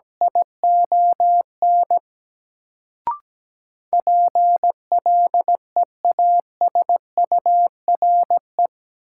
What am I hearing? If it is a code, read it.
Morse code, heard as a keyed tone, about 17 wpm, 700 Hz. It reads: ION PLEASURE
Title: Sentences from Top 1600 Words - New Word Drill - 17wpm